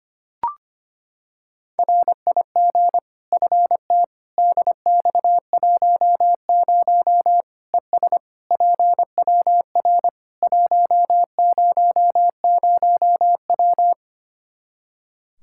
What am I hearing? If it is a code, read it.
Morse code, heard as a keyed tone, about 25 wpm, 700 Hz. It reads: RIG FT DX10 ES PWR 100W
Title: Practice QSO - Exchange 2 - Info - 25wpm